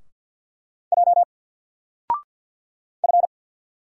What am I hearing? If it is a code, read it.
Morse code, heard as a keyed tone, about 50 wpm, 700 Hz. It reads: J F
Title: Single Character (Letter:Number:Punctuation:Prosign) 50wpm